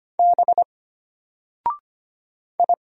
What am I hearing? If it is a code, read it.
Morse code, heard as a keyed tone, about 25 wpm, 700 Hz. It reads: B I
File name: Single-Letter-Number-25wpm